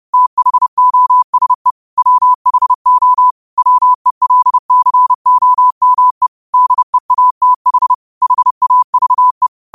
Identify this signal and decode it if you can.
Morse code, heard as a keyed tone, about 30 wpm, 1000 Hz. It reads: THOSE WHO WELCOME DEATH HAVE